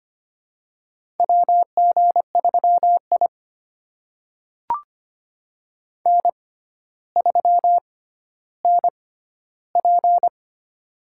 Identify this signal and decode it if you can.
Morse code, heard as a keyed tone, about 25 wpm, spaced as for 13 wpm, 700 Hz. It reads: WG3I N3NP
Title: USA Call Signs - 4 Characters - 6x Character Spacing - 25wpm